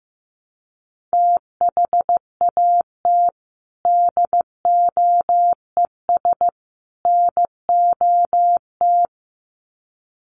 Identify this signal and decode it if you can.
Morse code, heard as a keyed tone, about 15 wpm, 700 Hz. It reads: THAT DOES NOT
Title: Sets of 3 Words 15wpm